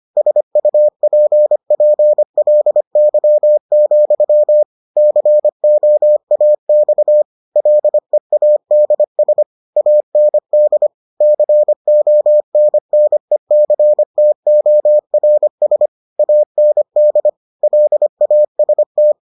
Morse code, heard as a keyed tone, about 25 wpm, 600 Hz. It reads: SUPPLY, COAX LEADS AND CONNECTORS AND LAST